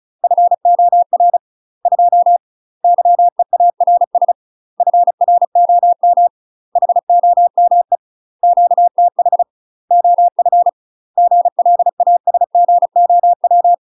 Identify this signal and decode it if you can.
Morse code, heard as a keyed tone, about 36 wpm, 700 Hz. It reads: FOR 2 YEARS FROM HOME QTH OF GLASGOW
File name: UK MASTER QSOs TRAINING / 35WPM / UK MASTER QSOs 0044